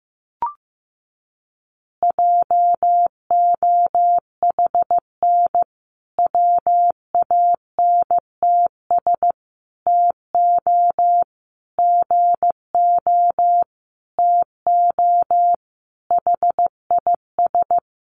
Morse code, heard as a keyed tone, about 15 wpm, 700 Hz. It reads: JOHN WANTS TO GO TO HIS
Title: Additional Easy Phrases 15wpm